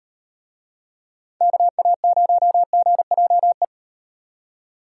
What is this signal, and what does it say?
Morse code, heard as a keyed tone, about 36 wpm, 700 Hz. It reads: KA0GJE